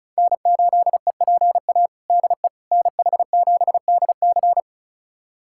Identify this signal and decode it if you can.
Morse code, heard as a keyed tone, about 36 wpm, 700 Hz. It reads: N8EPA DE NH7DC
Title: USA Callsign DE Callsign - 5 Characters - 35wpm